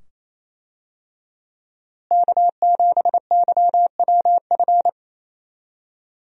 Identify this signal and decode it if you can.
Morse code, heard as a keyed tone, about 28 wpm, 700 Hz. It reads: K7YWF
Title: USA Call Signs - 5 Characters - Repeated 2x - 28wpm